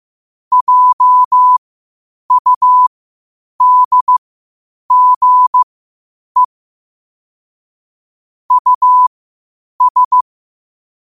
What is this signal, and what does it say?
Morse code, heard as a keyed tone, about 15 wpm, spaced as for 8 wpm, 1000 Hz. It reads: JUDGE US
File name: qotdfiles / 8wpm